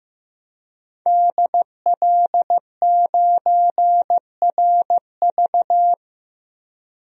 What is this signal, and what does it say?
Morse code, heard as a keyed tone, about 16 wpm, 700 Hz. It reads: DL9RV
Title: DXCC-100 Call Signs - 5 Characters - Repeated 2x - 15wpm